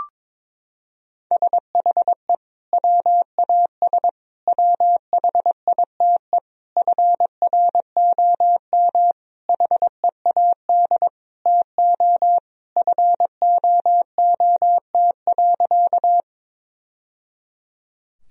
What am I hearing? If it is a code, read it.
Morse code, heard as a keyed tone, about 22 wpm, 700 Hz. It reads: SHE WAS WHITE FROM HEAD TO FOOT.
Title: Sentences from Top 1000 Words - Review Part 3 - 22wpm